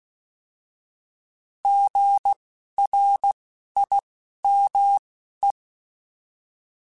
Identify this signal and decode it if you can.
Morse code, heard as a keyed tone, about 16 wpm, spaced as for 11 wpm, 800 Hz. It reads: GRIME
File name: Alphabetical / Alphabetical mp3s / Gs @10